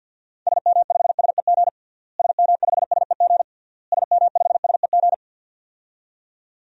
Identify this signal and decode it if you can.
Morse code, heard as a keyed tone, about 50 wpm, 700 Hz. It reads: SM5SEG SM5SEG SM5SEG
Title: DXCC-100 Call Signs - 6 Characters - Repeated 3x - 50wpm